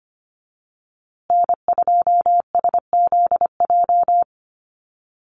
Morse code, heard as a keyed tone, about 25 wpm, 700 Hz. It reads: N2SZJ
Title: USA Call Signs Repeated 2x 25wpm